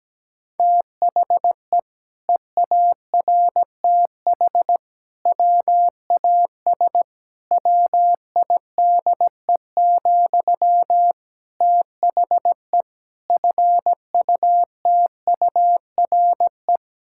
Morse code, heard as a keyed tone, about 17 wpm, 700 Hz. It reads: THE EARTH WAS WIDE, THE FUTURE
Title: Sentences from Top 1000 Words - Review Part 6 - 17wpm